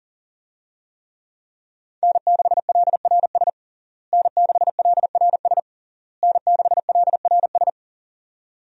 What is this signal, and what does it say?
Morse code, heard as a keyed tone, about 40 wpm, 700 Hz. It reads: N6LRS N6LRS N6LRS
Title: USA Call Signs - 5 Characters - Repeated 3x - 40wpm